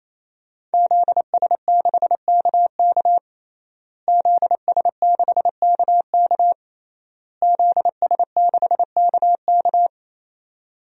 Morse code, heard as a keyed tone, about 28 wpm, 700 Hz. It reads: ZS6KK ZS6KK ZS6KK
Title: DXCC-100 Call Signs - 5 Characters - Repeated 3x - 28wpm